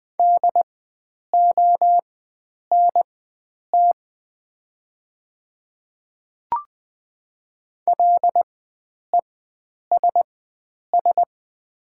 Morse code, heard as a keyed tone, about 20 wpm, spaced as for 9 wpm, 700 Hz. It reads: DONT LESS
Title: 4x Character Spacing - Top 500 Words - 20wpm